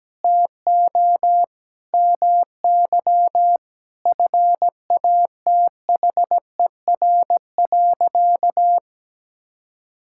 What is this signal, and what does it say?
Morse code, heard as a keyed tone, about 17 wpm, 700 Hz. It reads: TO MY FATHER.
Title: Sentences from Top 1000 Words - Review Part 5 - 17wpm